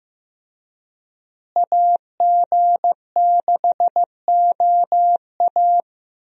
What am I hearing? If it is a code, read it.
Morse code, heard as a keyed tone, about 15 wpm, 700 Hz. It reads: AG6OA
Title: USA Call Signs - 5 Characters - Repeated 2x - 15wpm